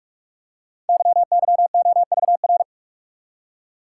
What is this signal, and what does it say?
Morse code, heard as a keyed tone, about 44 wpm, 700 Hz. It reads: YYOVR
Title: Code Groups - 5 Letters - 45wpm